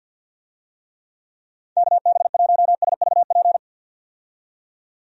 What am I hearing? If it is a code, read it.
Morse code, heard as a keyed tone, about 52 wpm, 700 Hz. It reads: KB1SVP
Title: USA Call Signs - 6 Characters - 50wpm